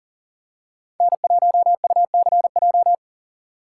Morse code, heard as a keyed tone, about 40 wpm, 700 Hz. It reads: N1UCJ
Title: USA Call Signs - 5 Characters - 3x Character Spacing - 40wpm